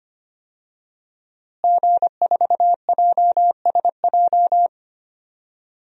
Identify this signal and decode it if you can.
Morse code, heard as a keyed tone, about 25 wpm, 700 Hz. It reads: G4JSJ
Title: DXCC-100 Call Signs - 5 Characters - Repeated 2x - 25wpm